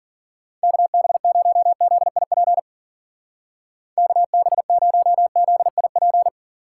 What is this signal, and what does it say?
Morse code, heard as a keyed tone, about 45 wpm, 700 Hz. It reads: KB0ZIP KB0ZIP
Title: Speed Racing - USA Call Signs 40wpm